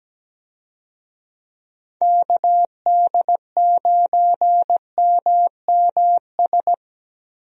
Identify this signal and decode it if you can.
Morse code, heard as a keyed tone, about 17 wpm, 700 Hz. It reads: KD9MMS
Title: USA Call Signs - 6 Characters - Repeated 3x - 17wpm